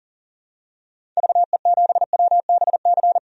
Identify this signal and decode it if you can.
Morse code, heard as a keyed tone, about 40 wpm, 700 Hz. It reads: VE7WBC
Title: DXCC-100 Call Signs - 6 Characters - 40wpm